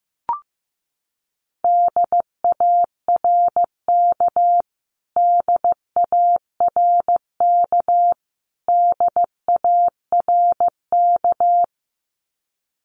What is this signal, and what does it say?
Morse code, heard as a keyed tone, about 15 wpm, 700 Hz. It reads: DARK DARK DARK
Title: Top 500 Words Repeated 15wpm